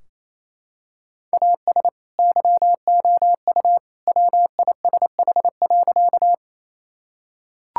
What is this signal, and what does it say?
Morse code, heard as a keyed tone, about 28 wpm, 700 Hz. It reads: AS YOU WISH.
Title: Sentences from Top 1600 Words 28wpm